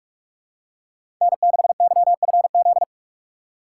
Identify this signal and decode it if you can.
Morse code, heard as a keyed tone, about 46 wpm, 700 Hz. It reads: N6YFZ
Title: USA Call Signs - 5 Characters - 4x Character Spacing - 45wpm